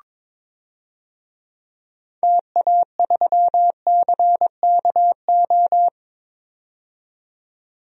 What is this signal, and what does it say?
Morse code, heard as a keyed tone, about 22 wpm, 700 Hz. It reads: TA3CKO
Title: DXCC-100 Call Signs - 6 Characters - 22wpm